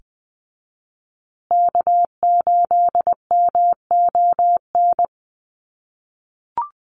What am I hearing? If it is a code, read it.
Morse code, heard as a keyed tone, about 20 wpm, 700 Hz. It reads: K8MON
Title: USA Call Signs Repeated 2x 20wpm